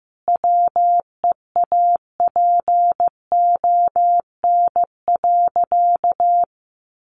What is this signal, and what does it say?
Morse code, heard as a keyed tone, about 15 wpm, 700 Hz. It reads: WEAPON.
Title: A Sound Of Thunder 15wpm